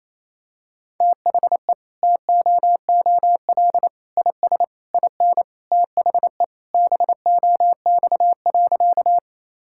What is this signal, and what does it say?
Morse code, heard as a keyed tone, about 28 wpm, 700 Hz. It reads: THE TOOL IS IN THE BOX.
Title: Sentences from Top 1000 Words - Review Part 6 - 28wpm